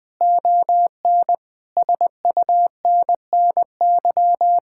Morse code, heard as a keyed tone, about 20 wpm, 700 Hz. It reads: ON SUNNY
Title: Difficult Sentences with Hints - 20wpm